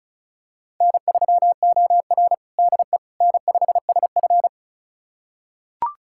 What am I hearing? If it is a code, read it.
Morse code, heard as a keyed tone, about 34 wpm, 700 Hz. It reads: N3OR DE N5SF
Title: USA Callsign DE Callsign - 4 Characters - 35wpm